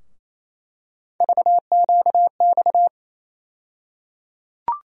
Morse code, heard as a keyed tone, about 28 wpm, 700 Hz. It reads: VQX
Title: USA Call Sign Postfixes - 3 Letters 28wpm